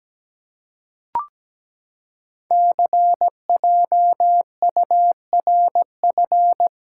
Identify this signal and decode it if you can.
Morse code, heard as a keyed tone, about 17 wpm, 700 Hz. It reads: CJURF
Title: Code Groups - 5 Letters - 17wpm